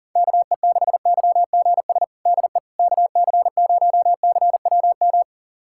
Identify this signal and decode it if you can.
Morse code, heard as a keyed tone, about 40 wpm, 700 Hz. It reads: KE6YGS DE KC0CWM
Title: USA Callsign DE Callsign 40wpm